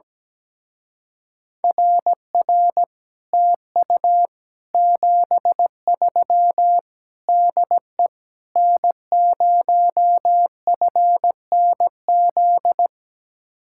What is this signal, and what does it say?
Morse code, heard as a keyed tone, about 18 wpm, 700 Hz. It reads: RR TU 73 DE N0FNZ
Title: Speed-Racing Common QSO Phrases - 17wpm